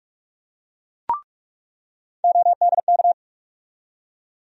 Morse code, heard as a keyed tone, about 42 wpm, 700 Hz. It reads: ODK
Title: USA Call Sign Postfixes - 3 Letters 45wpm